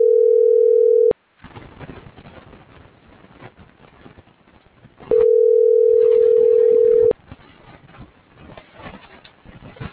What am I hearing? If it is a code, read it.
Ringback tone (North America), 440 + 480 Hz.